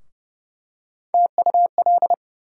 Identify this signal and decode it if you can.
Morse code, heard as a keyed tone, about 30 wpm, 700 Hz. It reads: TUL
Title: USA Call Sign Postfixes - 3 Letters 30wpm